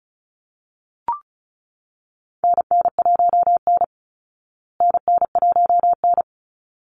Morse code, heard as a keyed tone, about 34 wpm, 700 Hz. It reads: NN1N NN1N
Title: Call Signs Repeated 3x - 35wpm